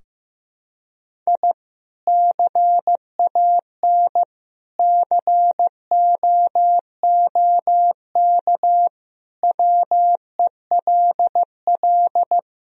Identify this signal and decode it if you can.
Morse code, heard as a keyed tone, about 16 wpm, 700 Hz. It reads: I CAN COOK WELL
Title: Sentences from Top 1000 Words - Review Part 4 - 15wpm